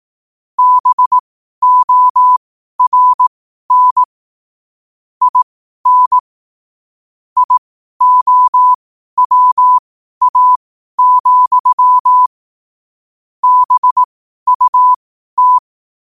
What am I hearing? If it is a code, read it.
Morse code, heard as a keyed tone, about 18 wpm, spaced as for 12 wpm, 1000 Hz. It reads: BORN IN IOWA, BUT
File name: qotdfiles / 12wpm